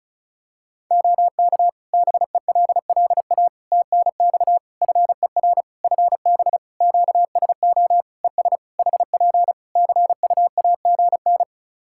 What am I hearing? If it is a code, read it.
Morse code, heard as a keyed tone, about 34 wpm, 700 Hz. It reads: OK BELLA TNX FER FB QSO ES HP CUAGN
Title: K1USN SST QSO Practice 35wpm